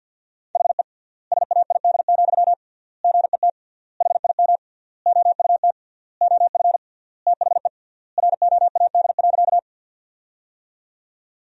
Morse code, heard as a keyed tone, about 50 wpm, 700 Hz. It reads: HE SAID, GET HIM OUT OF THE ROAD.
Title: 3x Word-Spacing - Sentences from Top 400 Words - 50wpm